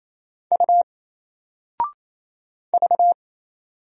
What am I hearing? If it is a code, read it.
Morse code, heard as a keyed tone, about 27 wpm, 700 Hz. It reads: U V
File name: Single-Letter-Number-28wpm